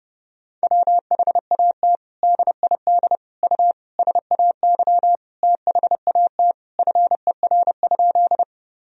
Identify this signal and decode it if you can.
Morse code, heard as a keyed tone, about 30 wpm, 700 Hz. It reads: WHAT DID U SAY THAT FER?
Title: Sentences from Top 200 Words and Abbreviations 30wpm